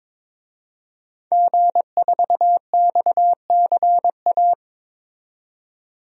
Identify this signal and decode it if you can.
Morse code, heard as a keyed tone, about 22 wpm, 700 Hz. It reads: G4XCA
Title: DXCC-100 Call Signs - 5 Characters - 22wpm